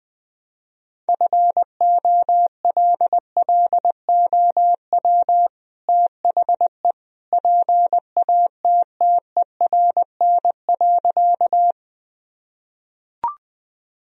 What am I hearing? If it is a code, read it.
Morse code, heard as a keyed tone, about 21 wpm, 700 Hz. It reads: FOLLOW THE PATTERN.
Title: Sentences from Top 1000 Words - Review Part 2 - 20wpm